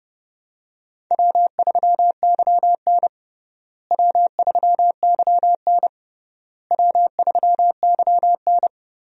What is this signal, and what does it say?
Morse code, heard as a keyed tone, about 32 wpm, 700 Hz. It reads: W3YN W3YN W3YN
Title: USA Call Signs - 4 Characters - Repeated 3x - 30wpm